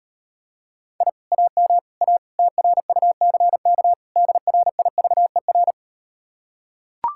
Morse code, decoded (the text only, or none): I AM A TRUCK DRIVER